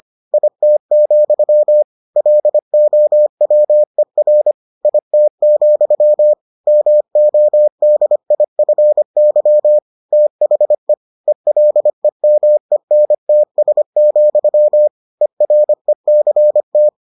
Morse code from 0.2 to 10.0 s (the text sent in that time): IT, LOWER IT, MODIFY